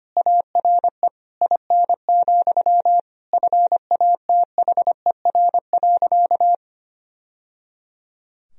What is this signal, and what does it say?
Morse code, heard as a keyed tone, about 25 wpm, 700 Hz. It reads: ARE IN, FATHER.
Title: Sentences from Top 1000 Words - Review Part 2 - 25wpm